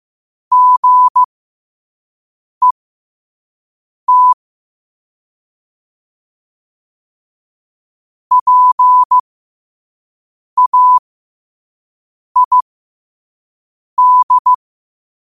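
Morse code, heard as a keyed tone, about 15 wpm, spaced as for 5 wpm, 1000 Hz. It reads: GET PAID
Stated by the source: Morse code sent at 5 wpm.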